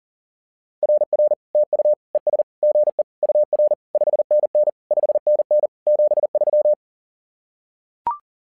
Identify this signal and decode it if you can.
Morse code, heard as a keyed tone, about 40 wpm, 600 Hz. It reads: RR TU ES GE UR 5NN 5NN 73
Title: SOTA QSO Practice 40wpm